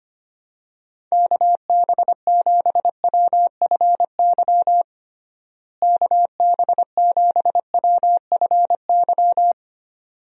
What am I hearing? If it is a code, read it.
Morse code, heard as a keyed tone, about 25 wpm, 700 Hz. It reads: KB7WFY KB7WFY
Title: USA Call Signs - 6 Characters - Repeated 2x - 25wpm